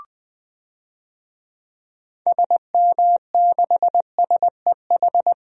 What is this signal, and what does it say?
Morse code, heard as a keyed tone, about 20 wpm, 700 Hz. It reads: SM6SEH
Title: DXCC-100 Call Signs - 6 Characters - Repeated 2x - 20wpm